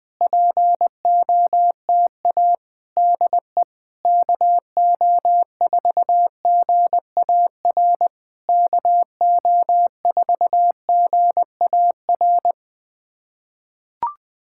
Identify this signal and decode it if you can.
Morse code, heard as a keyed tone, about 20 wpm, 700 Hz. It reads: POTA DE KO4GAR KO4GAR
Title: POTA QSO Practice - 20wpm